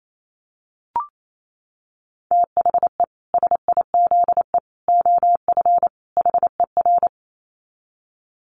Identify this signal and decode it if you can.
Morse code, heard as a keyed tone, about 28 wpm, 700 Hz. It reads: THE SIZE OF HER
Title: Dit Challenge 28wpm